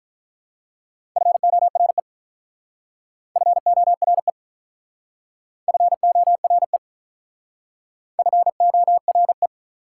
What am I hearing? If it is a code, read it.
Morse code, heard as a keyed tone, about 44 wpm, 700 Hz. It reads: FORE FORE FORE FORE